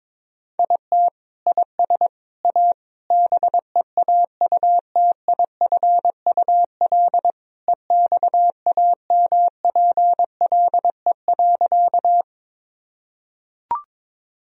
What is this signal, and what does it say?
Morse code, heard as a keyed tone, about 22 wpm, 700 Hz. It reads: IT IS A BEAUTIFUL EXAMPLE.
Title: Sentences from Top 1000 Words - Review Part 1 - 22wpm